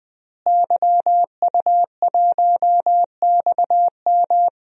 Morse code, heard as a keyed tone, about 20 wpm, 700 Hz. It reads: YU1XM
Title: DXCC-100 Call Signs - 5 Characters - Repeated 3x - 20wpm